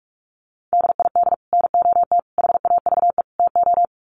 Morse code, heard as a keyed tone, about 44 wpm, 700 Hz. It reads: DID NOT HAVE TO
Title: Sets of 4 Words 45wpm